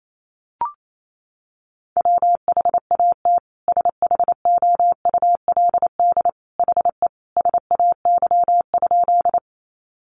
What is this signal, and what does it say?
Morse code, heard as a keyed tone, about 28 wpm, 700 Hz. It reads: WHAT SHOULD HE SAY?
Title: Sentences from Top 300 Words 28wpm